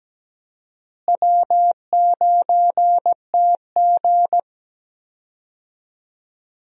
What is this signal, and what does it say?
Morse code, heard as a keyed tone, about 17 wpm, 700 Hz. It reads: W9TG
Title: USA Call Signs - 4 Characters - Repeated 2x - 17wpm